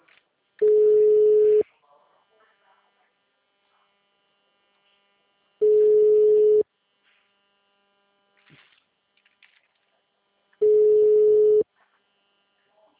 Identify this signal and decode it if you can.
Ringback tone (Europe, 425 Hz), with speech.